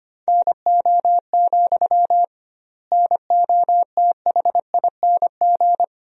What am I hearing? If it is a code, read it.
Morse code, heard as a keyed tone, about 25 wpm, 700 Hz. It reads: NO, NOTHING
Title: 2x Word-Spacing - Sentences from Top 500 Words - 25wpm